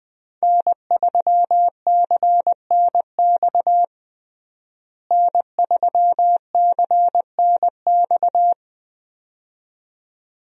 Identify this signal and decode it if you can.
Morse code, heard as a keyed tone, about 21 wpm, 700 Hz. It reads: N3CNX N3CNX
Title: USA Call Signs - 5 Characters - Repeated 2x - 20wpm